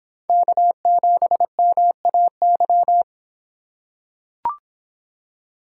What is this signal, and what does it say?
Morse code, heard as a keyed tone, about 26 wpm, 700 Hz. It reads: K7MAY